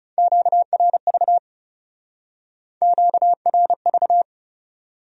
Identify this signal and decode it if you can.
Morse code, heard as a keyed tone, about 37 wpm, 700 Hz. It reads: QRV QRV
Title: Speed-Racing Q-Codes - 30wpm